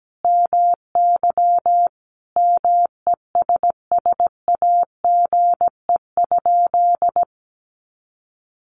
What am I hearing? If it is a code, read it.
Morse code, heard as a keyed tone, about 17 wpm, 700 Hz. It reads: MY MESSAGE?
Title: Sentences from Top 1100 Words 17wpm